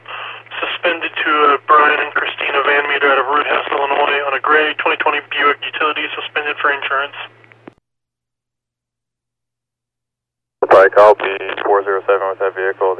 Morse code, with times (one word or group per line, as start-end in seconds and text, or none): none